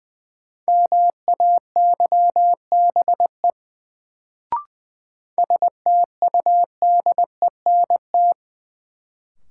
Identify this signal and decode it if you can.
Morse code, heard as a keyed tone, about 20 wpm, 700 Hz. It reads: MAYBE STUDENT
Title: Sentences from Top 1000 Words - New Word Drill - 20wpm